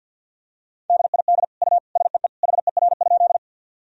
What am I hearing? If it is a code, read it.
Morse code, heard as a keyed tone, about 52 wpm, 700 Hz. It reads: DID U SEE HER?
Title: Sentences from Top 100 Words and Abbreviations 50wpm